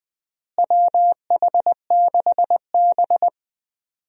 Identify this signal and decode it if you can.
Morse code, heard as a keyed tone, about 20 wpm, 700 Hz. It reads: WH6B
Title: USA Call Signs - 4 Characters - Repeated 2x - 20wpm